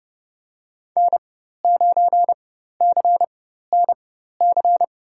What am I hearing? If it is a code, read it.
Morse code, heard as a keyed tone, about 32 wpm, spaced as for 14 wpm, 700 Hz. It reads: N9CNC
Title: USA Call Signs - 4x Character Spacing - 30wpm